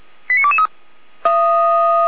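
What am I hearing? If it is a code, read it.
Five-tone selective call (ZVEI), heard as quick runs of five tones, tones of about 70 ms. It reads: 89223
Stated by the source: ZVEI five-tone selective call received on an SDR, logged by Feuerwehr Orlamünde fire brigade.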